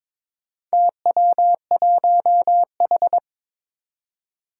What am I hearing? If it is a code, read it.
Morse code, heard as a keyed tone, about 22 wpm, 700 Hz. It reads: TW1H
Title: Four Challenging Letter-Number Combinations 22wpm